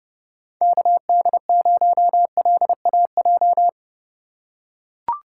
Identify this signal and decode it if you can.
Morse code, heard as a keyed tone, about 30 wpm, 700 Hz. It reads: KD0LAJ
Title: USA Call Signs - 4x Character Spacing - 30wpm